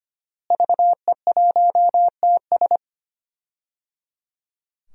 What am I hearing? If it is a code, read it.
Morse code, heard as a keyed tone, about 25 wpm, 700 Hz. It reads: VE1TS
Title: DXCC-100 Call Signs - 5 Characters - Repeated 3x - 25wpm